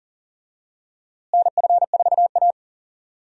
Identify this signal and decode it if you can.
Morse code, heard as a keyed tone, about 40 wpm, 700 Hz. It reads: NF4A
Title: USA Call Signs - 4 Characters - Repeated 2x - 40wpm